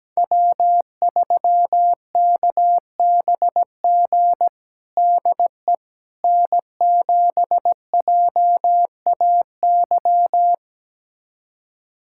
Morse code, heard as a keyed tone, about 17 wpm, 700 Hz. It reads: W3KBG DE N7JAY
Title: USA Callsign DE Callsign - 5 Characters - 17wpm